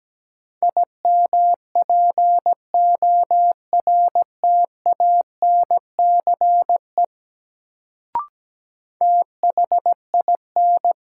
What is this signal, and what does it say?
Morse code, heard as a keyed tone, about 17 wpm, 700 Hz. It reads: IMPORTANCE THIN
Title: Sentences from Top 1200 Words - New Word Drill Alt - 17wpm